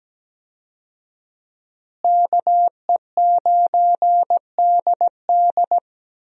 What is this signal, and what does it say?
Morse code, heard as a keyed tone, about 17 wpm, 700 Hz. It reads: KE9DD
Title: USA Call Signs - 5 Characters - Repeated 3x - 17wpm